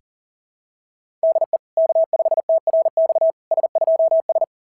Morse code, heard as a keyed tone, about 40 wpm, 690 Hz. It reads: DE K5TRX S2S
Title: SOTA QSO Practice 40wpm